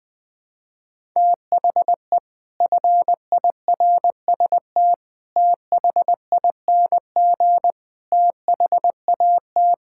Morse code, heard as a keyed tone, about 19 wpm, 700 Hz. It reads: THE FIRST THING THAT
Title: Sets of 8 Words - 20wpm